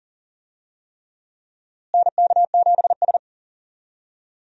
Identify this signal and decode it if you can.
Morse code, heard as a keyed tone, about 40 wpm, 700 Hz. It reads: NK7S
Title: USA Call Signs - 4 Characters - Repeated 3x - 40wpm